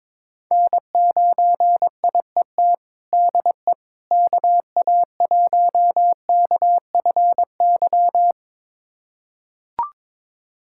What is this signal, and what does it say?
Morse code, heard as a keyed tone, about 22 wpm, 700 Hz. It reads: N9IET DE KA1KFY
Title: USA Callsign DE Callsign 22wpm